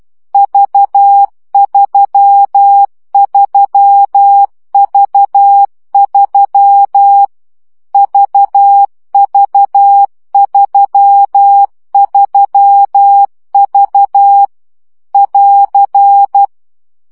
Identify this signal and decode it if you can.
Morse code, heard as a keyed tone, about 12 wpm, 800 Hz. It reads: V33V3 VV33V <AR>
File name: CW gyakorlas v 3 10 baud